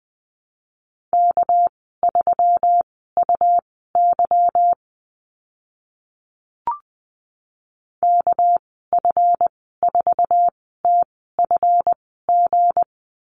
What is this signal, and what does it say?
Morse code, heard as a keyed tone, about 20 wpm, spaced as for 15 wpm, 700 Hz. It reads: K3UY KF4TFG
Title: 245 Lesson 39g - Letter z - Call Signs 20wpm